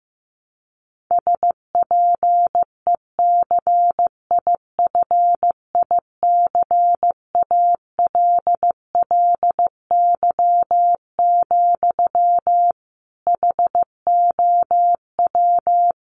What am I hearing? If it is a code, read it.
Morse code, heard as a keyed tone, about 15 wpm, 700 Hz. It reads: SPECIFICALLY, HOW
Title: Sentences from Top 1100 Words 15wpm